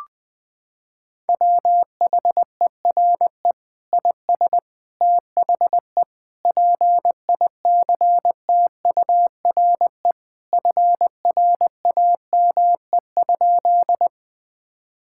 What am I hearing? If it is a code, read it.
Morse code, heard as a keyed tone, about 20 wpm, 700 Hz. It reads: WHERE IS THE PICTURE FRAME?
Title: Sentences from Top 1000 Words - Review Part 6 - 20wpm